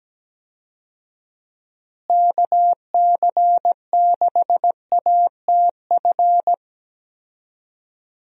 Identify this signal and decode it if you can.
Morse code, heard as a keyed tone, about 17 wpm, 700 Hz. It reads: KC6ATF
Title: USA Call Signs - 6 Characters - Repeated 3x - 17wpm